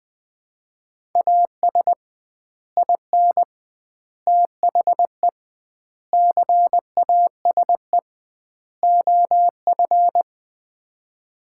Morse code, heard as a keyed tone, about 20 wpm, 700 Hz. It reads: AS IN THE CASE OF
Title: 2x Word-Spacing - Sets of 5 Words - 20wpm